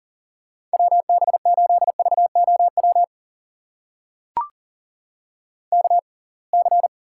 Morse code, heard as a keyed tone, about 40 wpm, spaced as for 30 wpm, 700 Hz. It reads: WB8VOW KC
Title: USA Call Signs - 6 Characters - 6x Character Spacing - 40wpm